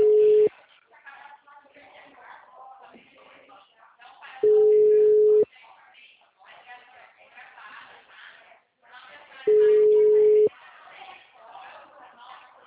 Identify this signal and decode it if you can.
Ringback tone (Europe, 425 Hz), with speech.